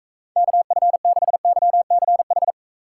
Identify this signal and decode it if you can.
Morse code, heard as a keyed tone, about 36 wpm, 700 Hz. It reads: KF6YCH